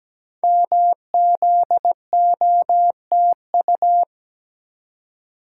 Morse code, heard as a keyed tone, about 17 wpm, 700 Hz. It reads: MZOTU
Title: Code Groups - 5 Letters - 17wpm